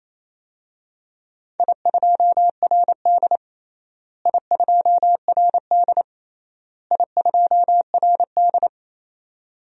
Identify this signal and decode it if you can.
Morse code, heard as a keyed tone, about 28 wpm, 700 Hz. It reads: I2RD I2RD I2RD
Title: DXCC-100 Call Signs - 4 Characters - Repeated 3x - 28wpm